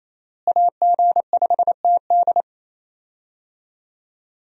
Morse code, heard as a keyed tone, about 28 wpm, 700 Hz. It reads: AG5TD
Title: USA Call Signs - 5 Characters - Repeated 2x - 28wpm